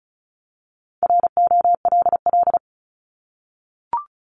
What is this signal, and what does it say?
Morse code, heard as a keyed tone, about 35 wpm, 700 Hz. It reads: ROLL
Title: Multi-letter Phonemes 35wpm